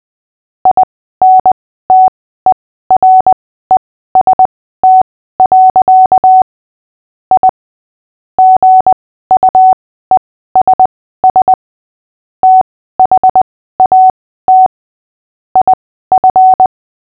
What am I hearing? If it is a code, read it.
Morse code, heard as a keyed tone, about 20 wpm, spaced as for 14 wpm, 750 Hz. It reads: INTEREST. I GUESS THAT IF
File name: cw / fists / Short-Story-8-14wpm